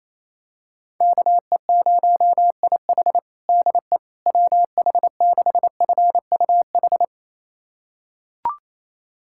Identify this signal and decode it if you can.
Morse code, heard as a keyed tone, about 28 wpm, 700 Hz. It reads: KE0IH DE WH6FUH
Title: USA Callsign DE Callsign 28wpm